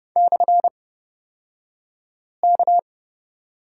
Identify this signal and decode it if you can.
Morse code, heard as a keyed tone, about 30 wpm, spaced as for 5 wpm, 700 Hz. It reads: /K
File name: Lesson 38d - Letter q - Two Characters 30wpm